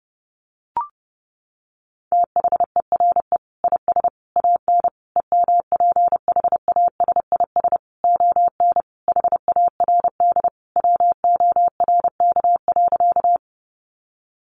Morse code, heard as a keyed tone, about 30 wpm, 700 Hz. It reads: THERE IS AN EMPHASIS ON HARD WORK.
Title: Sentences from Top 1400 Words 30wpm